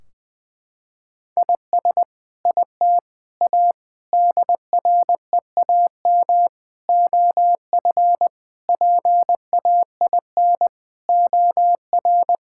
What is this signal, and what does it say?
Morse code, heard as a keyed tone, about 21 wpm, 700 Hz. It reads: IS IT A DREAM OF PAIN OR
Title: Sentences from Top 1600 Words 20wpm